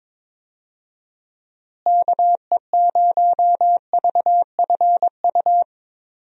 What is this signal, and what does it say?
Morse code, heard as a keyed tone, about 22 wpm, 700 Hz. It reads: KE0VFU
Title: USA Call Signs - 6 Characters - Repeated 2x - 22wpm